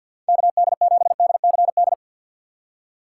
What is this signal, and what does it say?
Morse code, heard as a keyed tone, about 48 wpm, 690 Hz. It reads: KD7DCD
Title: USA Call Signs - 6 Characters - 3x Character Spacing - 50wpm